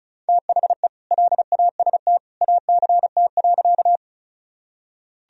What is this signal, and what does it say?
Morse code, heard as a keyed tone, about 34 wpm, 700 Hz. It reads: THE LAST ACT.
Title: Sentences from Top 1000 Words - Review Part 1 - 35wpm